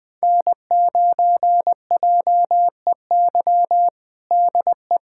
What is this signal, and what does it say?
Morse code, heard as a keyed tone, about 20 wpm, 700 Hz. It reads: N9JEY DE
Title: USA Callsign DE Callsign - 5 Characters - 20wpm